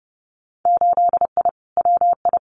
Morse code, heard as a keyed tone, about 30 wpm, 700 Hz. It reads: 8I WI
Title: Winter Field Day Exchanges 30wpm